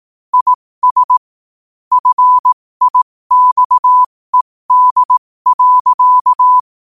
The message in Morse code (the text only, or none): IS FIXED.